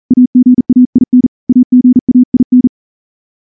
Morse code, heard as a keyed tone, about 38 wpm, 250 Hz. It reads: AGAIN AGAIN
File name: morse / morse-training-40WPM